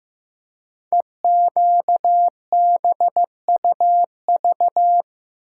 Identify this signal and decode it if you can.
Morse code, heard as a keyed tone, about 15 wpm, 700 Hz. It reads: EQBUV
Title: Code Groups - 5 Letters - 15wpm